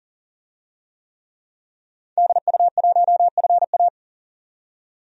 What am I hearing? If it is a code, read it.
Morse code, heard as a keyed tone, about 40 wpm, 700 Hz. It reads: DU1FA
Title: DXCC-100 Call Signs - 5 Characters - 40wpm